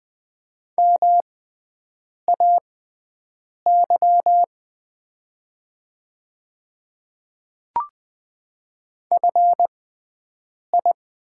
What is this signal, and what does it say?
Morse code, heard as a keyed tone, about 20 wpm, spaced as for 7 wpm, 700 Hz. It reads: MAY FI
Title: 6x Character Spacing - Top 500 Words - 20wpm